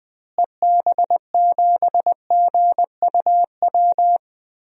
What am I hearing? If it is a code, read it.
Morse code, heard as a keyed tone, about 20 wpm, 700 Hz. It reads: EB7GUW
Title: DXCC-100 Call Signs - 6 Characters - Repeated 2x - 20wpm